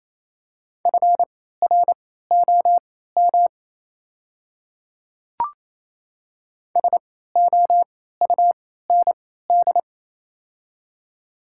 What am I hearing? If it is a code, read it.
Morse code, heard as a keyed tone, about 27 wpm, spaced as for 16 wpm, 700 Hz. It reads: FROM SOUND
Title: 3x Character Spacing - Top 100 Words - 28wpm